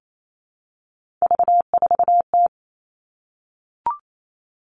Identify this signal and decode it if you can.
Morse code, heard as a keyed tone, about 28 wpm, 700 Hz. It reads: V4T
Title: All Cut Numbers - 3-Digit 28wpm